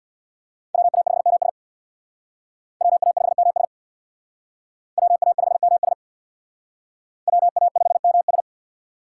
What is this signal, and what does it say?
Morse code, heard as a keyed tone, about 60 wpm, 700 Hz. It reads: WA5MS WA5MS WA5MS WA5MS
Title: Speed Racing - USA 2x2 Call Signs 50wpm